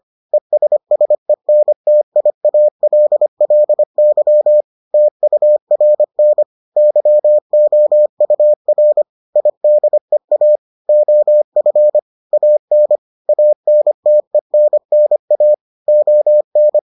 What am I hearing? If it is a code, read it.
Morse code, heard as a keyed tone, about 25 wpm, 600 Hz. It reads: ESSENTIALLY TURN YOUR IDEA OF AN ANTENNA ON